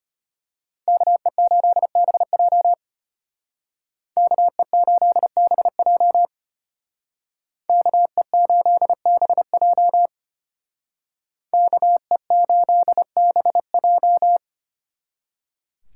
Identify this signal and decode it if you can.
Morse code, heard as a keyed tone, about 32 wpm, 700 Hz. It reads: KE8BJ KE8BJ KE8BJ KE8BJ